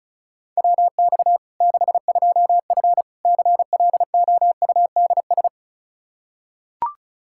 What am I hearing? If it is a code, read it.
Morse code, heard as a keyed tone, about 36 wpm, 700 Hz. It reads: WX 62F CLOUDS
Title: K1USN SST QSO Practice 35wpm